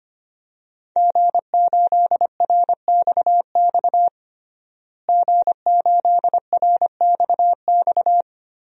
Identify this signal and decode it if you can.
Morse code, heard as a keyed tone, about 25 wpm, 700 Hz. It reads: G8RXX G8RXX
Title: DXCC-100 Call Signs - 5 Characters - Repeated 2x - 25wpm